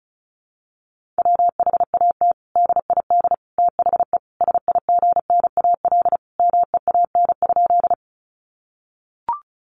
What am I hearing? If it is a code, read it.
Morse code, heard as a keyed tone, about 34 wpm, 700 Hz. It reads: WHAT DID THE SIGNAL MEAN?
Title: Sentences from Top 1400 Words 35wpm